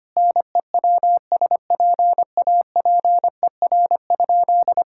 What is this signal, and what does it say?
Morse code, heard as a keyed tone, about 25 wpm, 700 Hz. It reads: NEWSPAPER?
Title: Sentences from Top 1000 Words - Review Part 6 - 25wpm